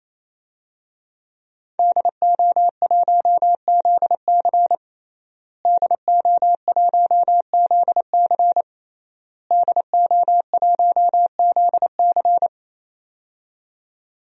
Morse code, heard as a keyed tone, about 27 wpm, 700 Hz. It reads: DO1ZC DO1ZC DO1ZC
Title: DXCC-100 Call Signs - 5 Characters - Repeated 3x - 28wpm